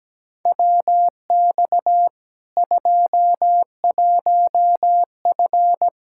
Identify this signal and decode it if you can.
Morse code, heard as a keyed tone, about 17 wpm, 700 Hz. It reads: WX 21F
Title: K1USN SST QSO Practice 17wpm